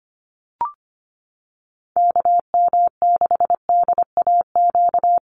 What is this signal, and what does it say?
Morse code, heard as a keyed tone, about 25 wpm, 700 Hz. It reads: KM6DAQ
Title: USA Call Signs Repeated 3x 25wpm